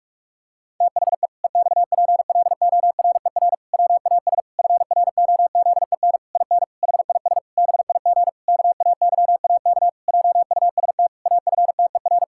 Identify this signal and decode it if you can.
Morse code, heard as a keyed tone, about 46 wpm, 700 Hz. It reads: THE EXPLORER WAS FROZEN IN HIS BIG KAYAK JUST AFTER
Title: Pangram Challenge v2 - 45wpm